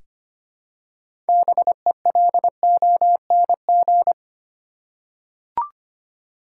Morse code, heard as a keyed tone, about 25 wpm, 700 Hz. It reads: BELONG
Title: Six Letter Words - 25wpm